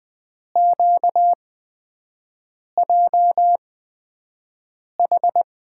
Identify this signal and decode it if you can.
Morse code, heard as a keyed tone, about 20 wpm, spaced as for 5 wpm, 700 Hz. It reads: QJH
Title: USA Call Signs - 5 Characters - 8x Character Spacing - 20wpm